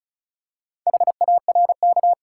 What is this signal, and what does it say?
Morse code, heard as a keyed tone, about 36 wpm, 700 Hz. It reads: HARK
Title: Pangram Challenge v2 - 35wpm